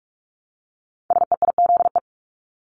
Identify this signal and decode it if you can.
Morse code, heard as a keyed tone, about 46 wpm, 700 Hz. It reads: SEIZE
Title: Multi-letter Phonemes Repeated 3x 45wpm